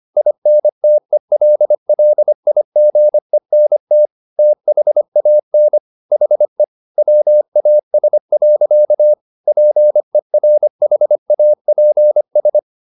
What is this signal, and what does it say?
Morse code, heard as a keyed tone, about 25 wpm, 600 Hz. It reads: INTELLIGENT THAN HE WAS. PERHAPS